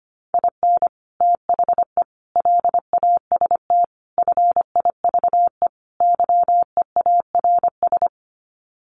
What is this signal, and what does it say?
Morse code, heard as a keyed tone, about 25 wpm, 700 Hz. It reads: IN THE LAST FIVE YEARS
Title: Sets of 5 Words Encore 25wpm